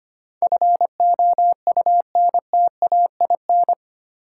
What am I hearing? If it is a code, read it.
Morse code, heard as a keyed tone, about 25 wpm, 700 Hz. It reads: FOUNTAIN
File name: practice / 25wpm / Easy Words with Hints - 25wpm